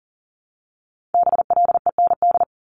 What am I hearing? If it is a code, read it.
Morse code, heard as a keyed tone, about 41 wpm, 700 Hz. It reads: BLEND
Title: Multi-letter Phonemes 40wpm